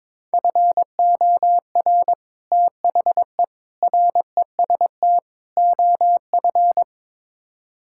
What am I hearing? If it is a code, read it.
Morse code, heard as a keyed tone, about 22 wpm, 700 Hz. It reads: FOR THE REST OF
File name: practice / 22wpm / Build A Phrase - 5-Word Phrases - 22wpm